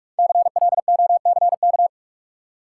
Morse code, heard as a keyed tone, about 44 wpm, 700 Hz. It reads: CLOCK
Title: Five Letter Words - 45wpm